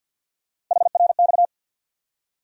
Morse code, heard as a keyed tone, about 50 wpm, 700 Hz. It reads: HRX
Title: USA Call Sign Postfixes - 3 Letters 50wpm